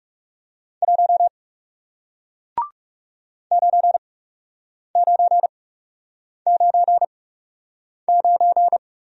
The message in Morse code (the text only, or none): 1 9999